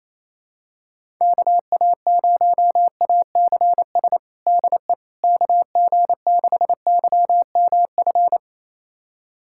Morse code, heard as a keyed tone, about 27 wpm, 700 Hz. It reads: KA0ACS DE KG6YMF
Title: USA Callsign DE Callsign 28wpm